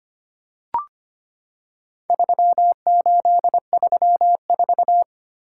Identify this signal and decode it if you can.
Morse code, heard as a keyed tone, about 25 wpm, 700 Hz. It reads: 3834
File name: practice / 25wpm / Four Letter-Number Combinations - 25wpm